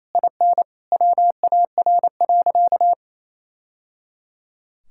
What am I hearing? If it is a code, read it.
Morse code, heard as a keyed tone, about 28 wpm, 700 Hz. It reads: IN WAR.
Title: Sentences from Top 1000 Words - Review Part 6 - 28wpm